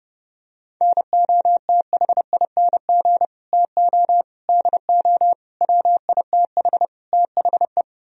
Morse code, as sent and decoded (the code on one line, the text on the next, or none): -. --- - .... .. -. --. / - --- / -.. --- / .-- .. - .... / - .... .
NOTHING TO DO WITH THE